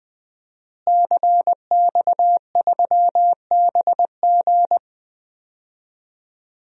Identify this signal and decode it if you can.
Morse code, heard as a keyed tone, about 20 wpm, 700 Hz. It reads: CX3BG
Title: DXCC-100 Call Signs - 5 Characters - Repeated 2x - 20wpm